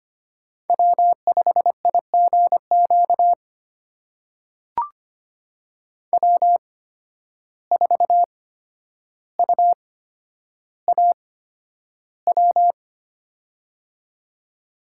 Morse code, heard as a keyed tone, about 25 wpm, spaced as for 11 wpm, 700 Hz. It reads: W5IGQ W4UAW
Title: USA Call Signs - 8x Character Spacing - 25wpm